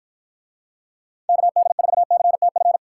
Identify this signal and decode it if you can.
Morse code, heard as a keyed tone, about 54 wpm, 700 Hz. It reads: KD4CTF